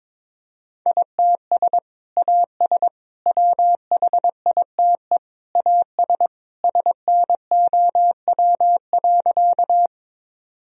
Morse code, heard as a keyed tone, about 22 wpm, 700 Hz. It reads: ITS AS WHITE AS SNOW.
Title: Sentences from Top 1000 Words - Review Part 3 - 22wpm